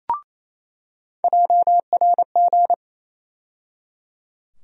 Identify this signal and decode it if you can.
Morse code, heard as a keyed tone, about 28 wpm, 700 Hz. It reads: JRG
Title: USA Call Sign Postfixes - 3 Letters 28wpm